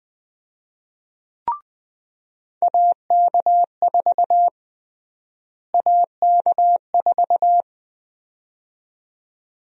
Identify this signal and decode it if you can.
Morse code, heard as a keyed tone, about 20 wpm, 700 Hz. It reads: AK4 AK4
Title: USA Call Sign Prefixes - Repeated 2x - 20wpm